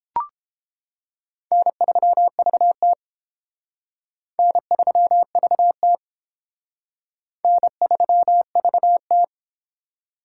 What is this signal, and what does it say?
Morse code, heard as a keyed tone, about 30 wpm, 700 Hz. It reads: N3VT N3VT N3VT